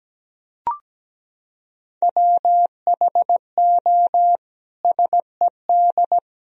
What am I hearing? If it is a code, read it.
Morse code, heard as a keyed tone, about 17 wpm, 700 Hz. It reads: WHO SED
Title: Sentences from Top 300 Words and Abbreviations 17wpm